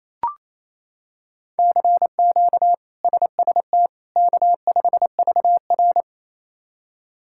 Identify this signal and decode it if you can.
Morse code, heard as a keyed tone, about 28 wpm, 700 Hz. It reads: CQ SST K5VR
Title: K1USN SST QSO Practice - 28wpm